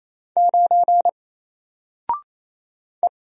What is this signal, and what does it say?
Morse code, heard as a keyed tone, about 28 wpm, 700 Hz. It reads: 9 E
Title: Single Letter-Number Alt 28wpm